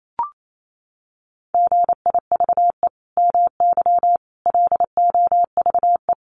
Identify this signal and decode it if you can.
Morse code, heard as a keyed tone, about 28 wpm, 700 Hz. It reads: GIVE MY LOVE
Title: Sentences from Top 1400 Words 28wpm